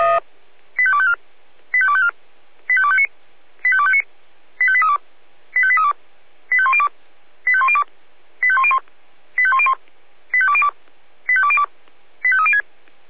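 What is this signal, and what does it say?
Five-tone selective call (ZVEI), heard as quick runs of five tones, tones of about 70 ms. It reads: 85254 85254 85259 85259 85832 85832 85102 85102 85101 85101 85202 85202 85306
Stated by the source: ZVEI five-tone selective call 85350 received by SDR.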